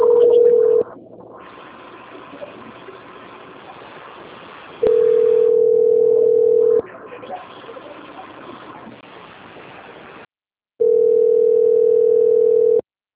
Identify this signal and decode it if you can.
Ringback tone (North America), 440 + 480 Hz.